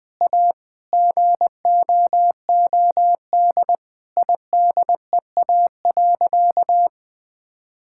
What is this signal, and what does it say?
Morse code, heard as a keyed tone, about 20 wpm, 700 Hz. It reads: A GOOD IDEA.
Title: Sentences with 5-Word Phrases - 20wpm